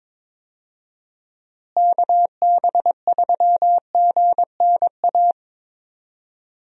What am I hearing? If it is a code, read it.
Morse code, heard as a keyed tone, about 21 wpm, 700 Hz. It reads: KB3GNA
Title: USA Call Signs - 6 Characters - Repeated 2x - 22wpm